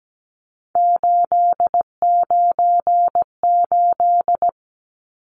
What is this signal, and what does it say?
Morse code, heard as a keyed tone, about 17 wpm, 700 Hz. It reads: 898
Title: Numbers - 3-Digit 17wpm